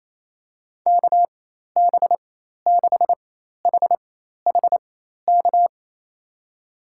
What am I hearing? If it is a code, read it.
Morse code, heard as a keyed tone, about 27 wpm, spaced as for 13 wpm, 700 Hz. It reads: KB6HHK
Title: USA Call Signs - 6 Characters - 4x Character Spacing - 28wpm